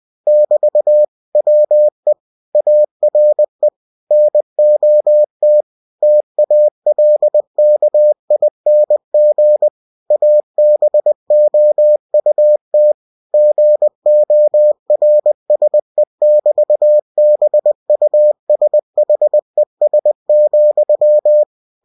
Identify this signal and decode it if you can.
Morse code, heard as a keyed tone, about 20 wpm, 600 Hz. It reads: <BT> WE ARE NOT TALKING ABOUT GORSE<BT>BUSHES,
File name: winnie-20wpm